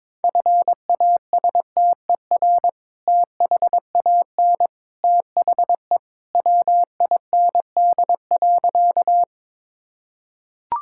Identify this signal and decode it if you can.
Morse code, heard as a keyed tone, about 22 wpm, 700 Hz. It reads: FASTER THAN THE WIND.
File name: practice / 22wpm / Sentences with 2-Word Phrases - 22wpm